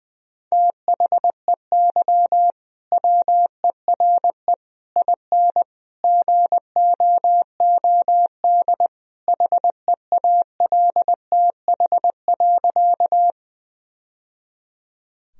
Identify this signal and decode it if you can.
Morse code, heard as a keyed tone, about 20 wpm, 700 Hz. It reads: THEY WERE IN GOOD HEALTH.
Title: Sentences from Top 1000 Words - Review Part 3 - 20wpm